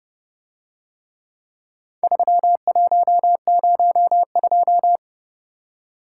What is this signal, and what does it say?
Morse code, heard as a keyed tone, about 30 wpm, 700 Hz. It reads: 3102